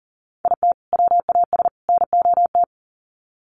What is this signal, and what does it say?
Morse code, heard as a keyed tone, about 40 wpm, 700 Hz. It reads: IT WAS NOT
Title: Sets of 3 Words 40wpm